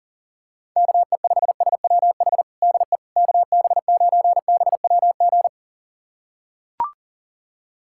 Morse code, heard as a keyed tone, about 40 wpm, 700 Hz. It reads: KE5SWH DE KB9BWG
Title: USA Callsign DE Callsign - 6 Characters - 40wpm